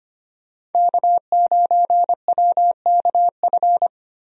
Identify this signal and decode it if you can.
Morse code, heard as a keyed tone, about 25 wpm, 700 Hz. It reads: K9WKF
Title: USA Call Signs - 5 Characters - 25wpm